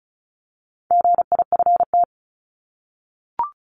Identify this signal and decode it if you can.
Morse code, heard as a keyed tone, about 34 wpm, 700 Hz. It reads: GIFT
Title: Top 751-1000 Words 35wpm